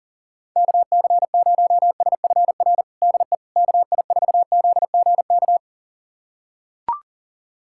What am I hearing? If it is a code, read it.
Morse code, heard as a keyed tone, about 40 wpm, 700 Hz. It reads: KC0SFR DE KI4ZGK
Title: USA Callsign DE Callsign - 6 Characters - 40wpm